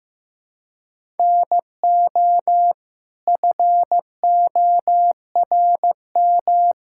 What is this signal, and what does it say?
Morse code, heard as a keyed tone, about 15 wpm, 700 Hz. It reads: NO FORM
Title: Sentences from Top 1000 Words - Review Part 6 - 15wpm